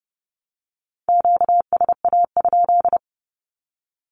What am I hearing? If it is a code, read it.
Morse code, heard as a keyed tone, about 30 wpm, 700 Hz. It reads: QSA?
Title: Q-Codes 30wpm